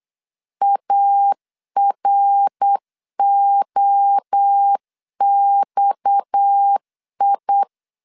Morse code, heard as a keyed tone, about 8 wpm, 800 Hz. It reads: AROXI
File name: LEZIONI / 6 PKX0 / 6 PKX0 8wpm 3341